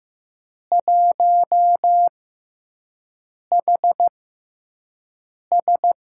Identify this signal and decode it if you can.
Morse code, heard as a keyed tone, about 15 wpm, spaced as for 5 wpm, 700 Hz. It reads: 1HS
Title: USA Call Signs - 4 Characters - 6x Character Spacing - 15wpm